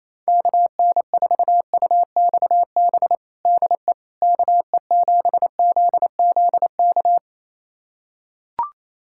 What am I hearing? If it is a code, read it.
Morse code, heard as a keyed tone, about 27 wpm, 700 Hz. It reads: KN4UXB DE KE7ZZK
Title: USA Callsign DE Callsign - 6 Characters - 28wpm